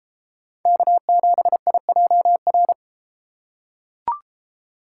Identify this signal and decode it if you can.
Morse code, heard as a keyed tone, about 31 wpm, 700 Hz. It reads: K7IJR